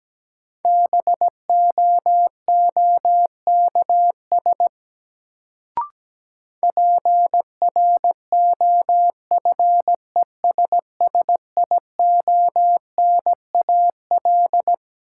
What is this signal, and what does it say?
Morse code, heard as a keyed tone, about 17 wpm, 700 Hz. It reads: BOOKS PROFESSIONAL
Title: Sentences from Top 1000 Words - Review Part 4 - New Word Drill - 17wpm